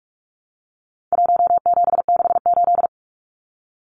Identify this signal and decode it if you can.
Morse code, heard as a keyed tone, about 44 wpm, 700 Hz. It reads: 1768
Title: Numbers - Contest Style - 4-Digit 45wpm